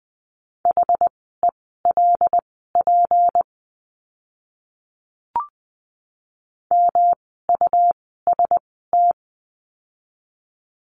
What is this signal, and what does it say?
Morse code, heard as a keyed tone, about 20 wpm, spaced as for 15 wpm, 700 Hz. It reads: HELP MUST
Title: 237 Lesson 38f - Letter q - Words 20wpm